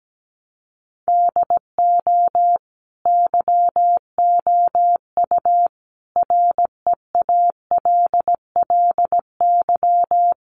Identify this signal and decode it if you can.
Morse code, heard as a keyed tone, about 17 wpm, 700 Hz. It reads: DO YOU REALLY
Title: Sentences from Top 600 Words 17wpm